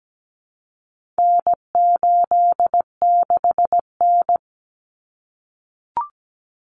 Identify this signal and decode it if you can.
Morse code, heard as a keyed tone, about 17 wpm, 700 Hz. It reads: N86N
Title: Common Cut Numbers - 4-Digit 17wpm